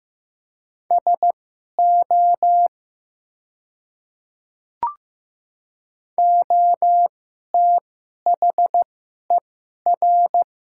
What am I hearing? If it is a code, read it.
Morse code, heard as a keyed tone, about 15 wpm, spaced as for 11 wpm, 700 Hz. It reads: SO OTHER
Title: 2x Character Spacing - Top 100 Words - 15wpm